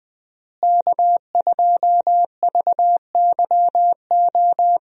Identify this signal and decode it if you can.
Morse code, heard as a keyed tone, about 21 wpm, 700 Hz. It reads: K2VYO
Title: USA Call Signs - 5 Characters - Repeated 3x - 20wpm